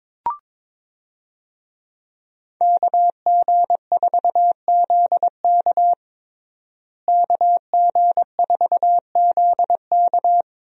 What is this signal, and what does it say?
Morse code, heard as a keyed tone, about 22 wpm, 700 Hz. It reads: KG4ZK KG4ZK
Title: USA Call Signs - 5 Characters - Repeated 3x - 22wpm